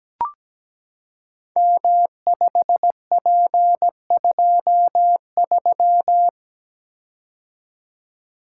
Morse code, heard as a keyed tone, about 17 wpm, 700 Hz. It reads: M5P23
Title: Code Groups - 5 Letter:Numbers - 17wpm